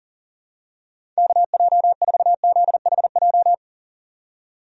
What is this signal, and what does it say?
Morse code, heard as a keyed tone, about 40 wpm, 700 Hz. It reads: KJ4ZHJ
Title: USA Call Signs - 6 Characters - 40wpm